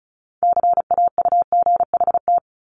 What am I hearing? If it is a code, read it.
Morse code, heard as a keyed tone, about 36 wpm, 700 Hz. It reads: CAUGHT
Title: Multi-letter Phonemes 35wpm